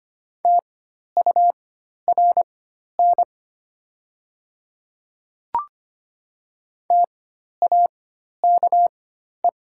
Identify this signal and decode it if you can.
Morse code, heard as a keyed tone, about 25 wpm, spaced as for 12 wpm, 690 Hz. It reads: TURN TAKE
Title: 4x Character Spacing - Top 300 Words - 25wpm